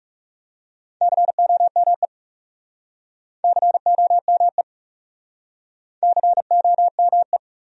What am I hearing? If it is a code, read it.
Morse code, heard as a keyed tone, about 43 wpm, 700 Hz. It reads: COME COME COME